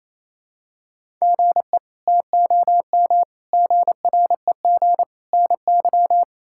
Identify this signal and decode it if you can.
Morse code, heard as a keyed tone, about 28 wpm, 700 Hz. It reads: GE TOM GREG NY
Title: K1USN SST QSO Practice - 28wpm